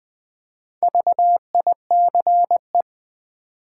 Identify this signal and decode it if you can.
Morse code, heard as a keyed tone, about 20 wpm, 700 Hz. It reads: VICE
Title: Easy Words with Hints - 20wpm